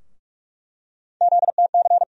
Morse code, heard as a keyed tone, about 47 wpm, 700 Hz. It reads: ZTC
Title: USA Call Sign Postfixes - 3 Letters 45wpm